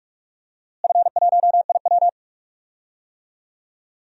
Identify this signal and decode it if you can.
Morse code, heard as a keyed tone, about 41 wpm, 700 Hz. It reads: F1IW
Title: DXCC-100 Call Signs - 4 Characters - Repeated 3x - 45wpm